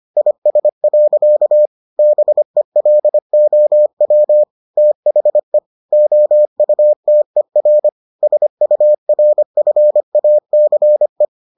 Morse code, heard as a keyed tone, about 25 wpm, 600 Hz. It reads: IS. BELOW THE OUTER SURFACE